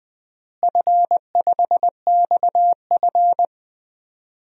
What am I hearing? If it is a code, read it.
Morse code, heard as a keyed tone, about 20 wpm, 700 Hz. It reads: F5XF
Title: DXCC-100 Call Signs - 4 Characters - Repeated 3x - 20wpm